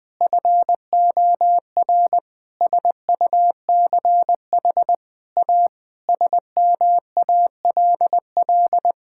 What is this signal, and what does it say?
Morse code, heard as a keyed tone, about 20 wpm, 700 Hz. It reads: FOR SUCH A SMALL
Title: Sentences from Top 1000 Words - Review Part 6 - 20wpm